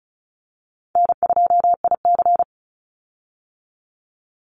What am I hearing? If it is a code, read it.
Morse code, heard as a keyed tone, about 34 wpm, 700 Hz. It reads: N2IC
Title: Easy Call Signs Repeated 2x - 35wpm